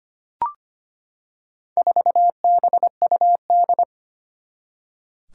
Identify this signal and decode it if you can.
Morse code, heard as a keyed tone, about 25 wpm, 700 Hz. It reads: 4BUD
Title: Four Challenging Letter-Number Combinations 25wpm